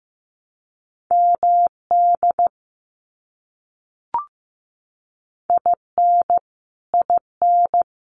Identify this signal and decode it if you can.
Morse code, heard as a keyed tone, about 15 wpm, 700 Hz. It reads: MD IN IN
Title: US States Repeated 3x 15wpm